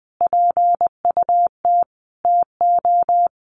Morse code, heard as a keyed tone, about 20 wpm, 700 Hz. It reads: PUT TO
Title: Sentences from Top 1400 Words 20wpm